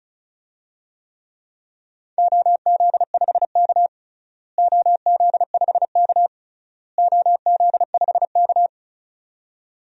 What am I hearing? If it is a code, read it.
Morse code, heard as a keyed tone, about 36 wpm, 700 Hz. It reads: OZ5K OZ5K OZ5K
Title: DXCC-100 Call Signs - 4 Characters - Repeated 3x - 35wpm